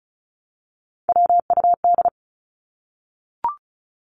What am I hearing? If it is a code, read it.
Morse code, heard as a keyed tone, about 34 wpm, 700 Hz. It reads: WUD
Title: Abbreviations 35wpm